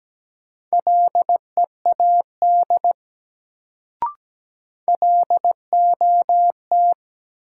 Morse code, heard as a keyed tone, about 17 wpm, 700 Hz. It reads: LEAD LOT
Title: Sentences from Top 500 Words - New Word Drill - 17wpm